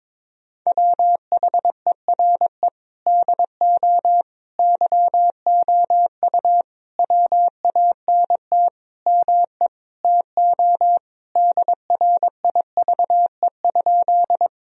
Morse code, heard as a keyed tone, about 22 wpm, 700 Hz. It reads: WHERE DO YOU WANT ME TO DRIVE?
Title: Sentences from Top 1000 Words - Review Part 2 - 22wpm